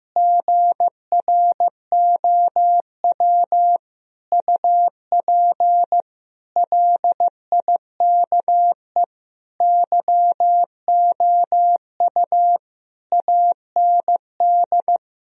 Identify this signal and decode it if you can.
Morse code, heard as a keyed tone, about 15 wpm, 700 Hz. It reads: GROW UP LIKE YOU AND
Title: Sentences from Top 1000 Words - Review Part 1 - 15wpm